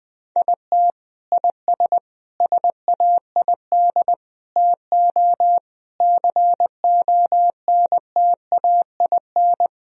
Morse code, heard as a keyed tone, about 20 wpm, 700 Hz. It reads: IT IS SAID TO CONTAIN
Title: Sentences from Top 1000 Words - Review Part 2 - 20wpm